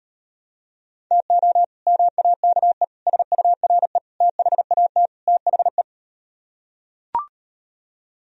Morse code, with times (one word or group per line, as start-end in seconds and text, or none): 1.1-1.6 TO
1.9-2.8 MAKE
3.1-4.0 SURE
4.2-5.1 THAT
5.3-5.8 THE